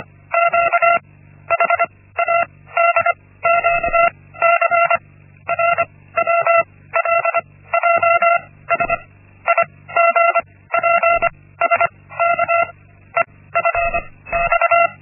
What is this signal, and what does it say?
Morse code, heard as a keyed tone, about 25 wpm, 680 Hz. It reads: Q H A D Q C R W L J S I G P S K E F X
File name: Linksmakalnis-1